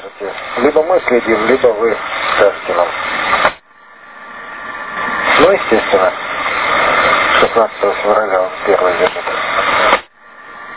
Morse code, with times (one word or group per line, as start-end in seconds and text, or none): none